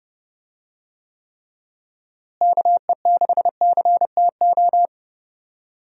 Morse code, heard as a keyed tone, about 30 wpm, 700 Hz. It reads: KE6CTO
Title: USA Call Signs - 6 Characters - Repeated 2x - 30wpm